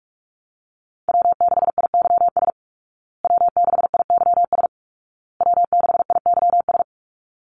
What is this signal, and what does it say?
Morse code, heard as a keyed tone, about 44 wpm, 700 Hz. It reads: W6IYS W6IYS W6IYS
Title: Call Signs Repeated 2x - 45wpm